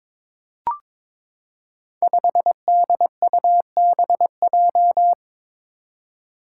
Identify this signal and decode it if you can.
Morse code, heard as a keyed tone, about 22 wpm, 700 Hz. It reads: 5DUBJ
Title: Five Challenging Letter-Number Combinations 22wpm